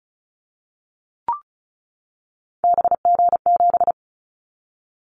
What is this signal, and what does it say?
Morse code, heard as a keyed tone, about 36 wpm, 700 Hz. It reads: BG7
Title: Callsign Prefixes for Top 100 DXCC - 35wpm